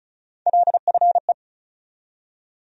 Morse code, heard as a keyed tone, about 34 wpm, 700 Hz. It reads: LFE
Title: USA Call Sign Postfixes - 3 Letters 35wpm